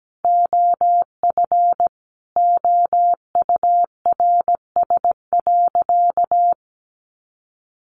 Morse code, heard as a keyed tone, about 17 wpm, 700 Hz. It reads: OF OURS.
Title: Sentences from Top 1300 Words 17wpm